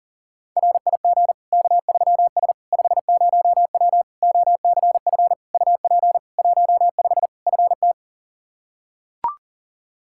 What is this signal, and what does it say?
Morse code, heard as a keyed tone, about 40 wpm, 700 Hz. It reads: RIG K3S 50W OCF UP 15 FT
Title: Practice QSO - Exchange 2 - Info - 40wpm